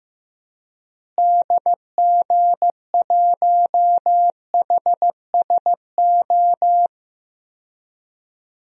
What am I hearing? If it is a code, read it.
Morse code, heard as a keyed tone, about 15 wpm, 700 Hz. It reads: DG1HSO
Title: DXCC-100 Call Signs - 6 Characters - Repeated 2x - 15wpm